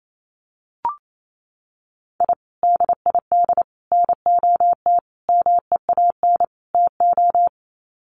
Morse code, heard as a keyed tone, about 28 wpm, 700 Hz. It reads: I DID NOT MEAN TO
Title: Sets of 5 Words 28wpm